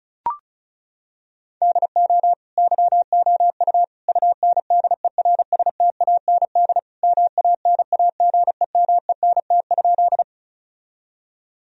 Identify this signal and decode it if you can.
Morse code, heard as a keyed tone, about 36 wpm, 700 Hz. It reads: DO YOU UNDERSTAND MANAGEMENT?
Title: Sentences from Top 1000 Words - Review Part 4 - 35wpm